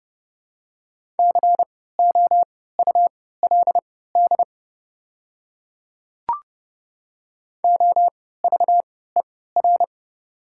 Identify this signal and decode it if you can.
Morse code, heard as a keyed tone, about 30 wpm, spaced as for 17 wpm, 700 Hz. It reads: COULD OVER
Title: 3x Character Spacing - Top 100 Words - 30wpm